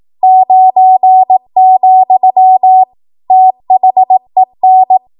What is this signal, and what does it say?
Morse code, heard as a keyed tone, about 18 wpm, 750 Hz. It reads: 9, THEN
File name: Archive / 18 WPM / 18WPM